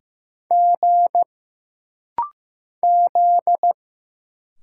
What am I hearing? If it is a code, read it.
Morse code, heard as a keyed tone, about 15 wpm, 700 Hz. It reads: G Z
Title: Single Letters - Rapid-Fire 15wpm